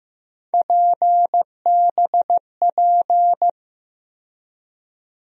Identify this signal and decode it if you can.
Morse code, heard as a keyed tone, about 15 wpm, 700 Hz. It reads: PBP
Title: USA Call Sign Postfixes - 3 Letters 15wpm